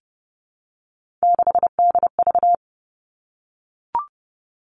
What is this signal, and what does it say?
Morse code, heard as a keyed tone, about 30 wpm, 700 Hz. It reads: 6DV
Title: All Cut Numbers - 3-Digit 30wpm